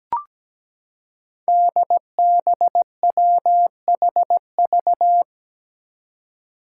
Morse code, heard as a keyed tone, about 17 wpm, 700 Hz. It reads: DBWHV
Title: Five Challenging Letter-Number Combinations 17wpm